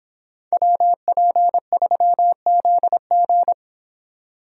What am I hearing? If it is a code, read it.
Morse code, heard as a keyed tone, about 26 wpm, 700 Hz. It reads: WP3ZG